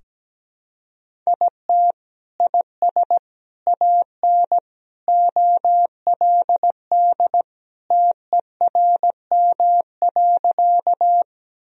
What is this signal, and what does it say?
Morse code, heard as a keyed tone, about 17 wpm, 700 Hz. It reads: IT IS AN OLD TERM.
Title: Sentences from Top 1000 Words - Review Part 4 - 17wpm